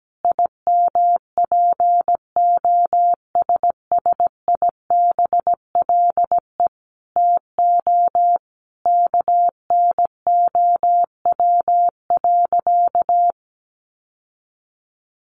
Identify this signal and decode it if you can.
Morse code, heard as a keyed tone, about 17 wpm, 700 Hz. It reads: IMPOSSIBLE TO KNOW.
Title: Sentences from Top 1100 Words 17wpm